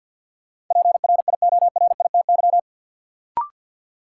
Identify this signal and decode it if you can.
Morse code, heard as a keyed tone, about 52 wpm, 700 Hz. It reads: PRIORITY
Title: Sentences from Top 1600 Words - New Word Drill - 50wpm